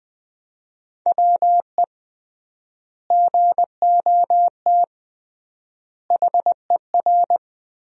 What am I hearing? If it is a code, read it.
Morse code, heard as a keyed tone, about 20 wpm, 700 Hz. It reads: WE GOT HER
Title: 3x Word-Spacing - Sentences from Top 400 Words - 20wpm